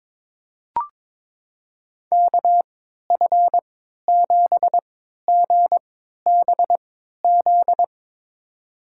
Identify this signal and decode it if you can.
Morse code, heard as a keyed tone, about 22 wpm, spaced as for 13 wpm, 700 Hz. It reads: KF7GBZ
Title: USA Call Signs - 6 Characters - 3x Character Spacing - 22wpm